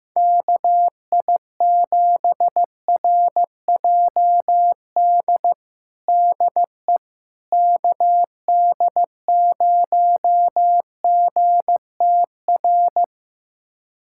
Morse code, heard as a keyed tone, about 16 wpm, 700 Hz. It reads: KI7RJD DE KD0GTR
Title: USA Callsign DE Callsign - 6 Characters - 15wpm